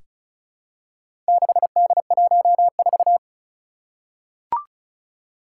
Morse code, heard as a keyed tone, about 36 wpm, 700 Hz. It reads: 6D14
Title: Four Challenging Letter-Number Combinations 35wpm